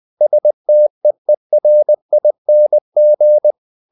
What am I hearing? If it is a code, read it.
Morse code, heard as a keyed tone, about 20 wpm, 600 Hz. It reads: STEERING
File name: the old man and the sea / the old man and the sea part 1 20wpm